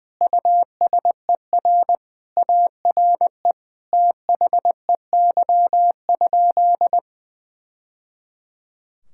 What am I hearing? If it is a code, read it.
Morse code, heard as a keyed tone, about 21 wpm, 700 Hz. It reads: USER ARE THEY?
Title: Sentences from Top 1000 Words - Review Part 5 - 20wpm